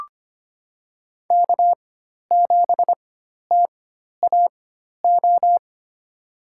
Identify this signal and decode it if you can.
Morse code, heard as a keyed tone, about 25 wpm, spaced as for 12 wpm, 700 Hz. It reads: K7TAO
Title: USA Call Signs - 5 Characters - 4x Character Spacing - 25wpm